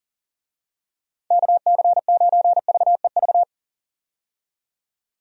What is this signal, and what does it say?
Morse code, heard as a keyed tone, about 40 wpm, 700 Hz. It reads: KC9VEV
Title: USA Call Signs - 6 Characters - 40wpm